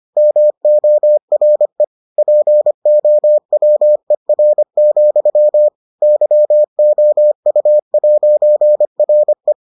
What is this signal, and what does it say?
Morse code, heard as a keyed tone, about 25 wpm, 600 Hz. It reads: MORE POWER, YOU'RE